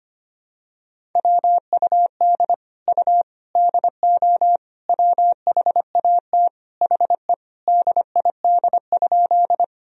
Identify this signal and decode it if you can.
Morse code, heard as a keyed tone, about 25 wpm, 700 Hz. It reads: WUD U DO WHAT HE DID?
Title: Sentences from Top 100 Words and Abbreviations 25wpm